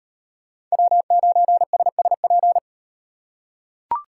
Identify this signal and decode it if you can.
Morse code, heard as a keyed tone, about 38 wpm, 700 Hz. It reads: W9SSP